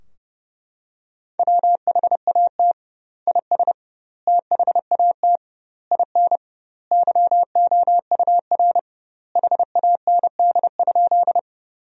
Morse code, heard as a keyed tone, about 30 wpm, 700 Hz. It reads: WHAT IS THAT IN YOUR HAND?
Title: 2x Word-Spacing - Sentences from Top 200 Words - 30wpm